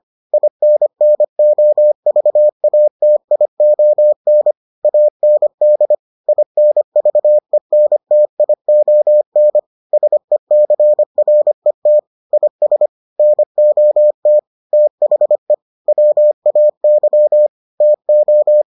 Morse code, heard as a keyed tone, about 25 wpm, 600 Hz. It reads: INNOVATION AND INVENTION SECRET IS NOT THE WAY TO